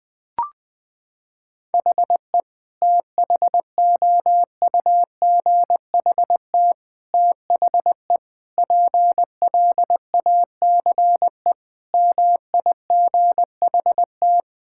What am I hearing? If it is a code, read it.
Morse code, heard as a keyed tone, about 20 wpm, 700 Hz. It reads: HE THOUGHT THE PLACE MIGHT
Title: Sentences from Top 1600 Words 20wpm